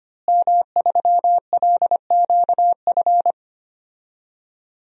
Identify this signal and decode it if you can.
Morse code, heard as a keyed tone, about 25 wpm, 700 Hz. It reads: M3LQF
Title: DXCC-100 Call Signs - 5 Characters - Repeated 2x - 25wpm